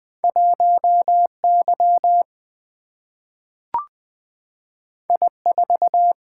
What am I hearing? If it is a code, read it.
Morse code, heard as a keyed tone, about 20 wpm, 700 Hz. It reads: 1 Y I 4
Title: Single-Letter Single-Number 20wpm